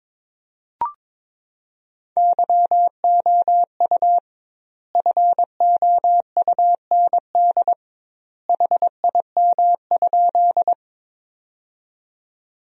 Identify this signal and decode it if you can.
Morse code, heard as a keyed tone, about 21 wpm, 700 Hz. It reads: YOU FOUND HIM?
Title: 2x Word-Spacing - Sentences from Top 300 Words - 22wpm